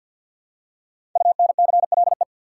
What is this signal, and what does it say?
Morse code, heard as a keyed tone, about 47 wpm, 700 Hz. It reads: UNCLE
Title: Five Letter Words - 50wpm